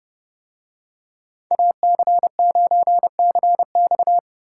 Morse code, heard as a keyed tone, about 30 wpm, 700 Hz. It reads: AC9CX